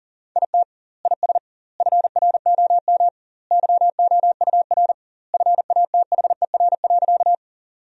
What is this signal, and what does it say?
Morse code, heard as a keyed tone, about 37 wpm, 700 Hz. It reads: IT IS FROM YOUR FATHER.
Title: 2x Word-Spacing - Sentences from Top 300 Words - 40wpm